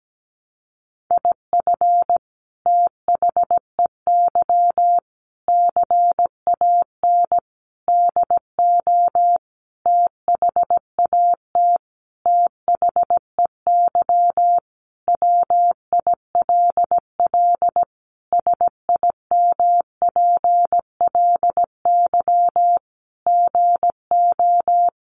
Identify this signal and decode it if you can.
Morse code, heard as a keyed tone, about 17 wpm, 700 Hz. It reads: IF THEY CAN DO THAT THEY WILL SIMPLY GO
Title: Sentences from Top 600 Words 17wpm